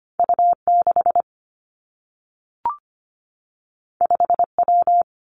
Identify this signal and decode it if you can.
Morse code, heard as a keyed tone, about 25 wpm, 700 Hz. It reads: U 6 5 W
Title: Two Challenging Letter-Number Combinations 25wpm